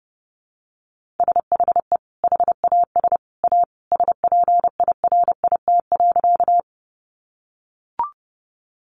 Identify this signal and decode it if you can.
Morse code, heard as a keyed tone, about 30 wpm, 700 Hz. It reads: SHE HAS A SPIRIT.
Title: Sentences from Top 1400 Words 30wpm